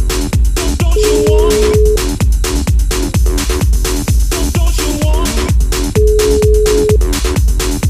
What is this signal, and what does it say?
Ringback tone (Europe, 425 Hz), with noise.